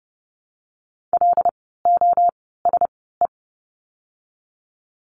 Morse code, heard as a keyed tone, about 30 wpm, spaced as for 17 wpm, 700 Hz. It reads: LOSE
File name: course / 30:10wpm / 108 Lesson 18f - Letter m - Words 30wpm